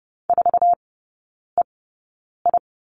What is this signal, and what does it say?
Morse code, heard as a keyed tone, about 30 wpm, spaced as for 9 wpm, 700 Hz. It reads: 4EI
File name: 081 Lesson 14e - Number 2 - Three Characters 30wpm